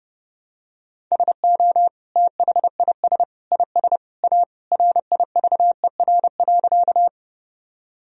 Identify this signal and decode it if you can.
Morse code, heard as a keyed tone, about 30 wpm, 700 Hz. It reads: SO THIS IS A RIVER.
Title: Sentences from Top 1000 Words - Review Part 1 - 30wpm